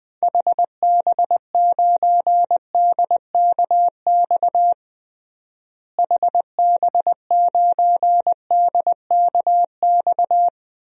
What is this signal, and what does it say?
Morse code, heard as a keyed tone, about 20 wpm, 700 Hz. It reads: HB9DKX HB9DKX
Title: DXCC-100 Call Signs - 6 Characters - Repeated 3x - 20wpm